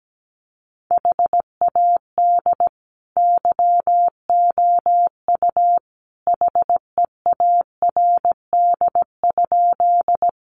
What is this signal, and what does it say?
Morse code, heard as a keyed tone, about 17 wpm, 700 Hz. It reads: HAD YOU HEARD?
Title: Sentences from Top 400 Words 17wpm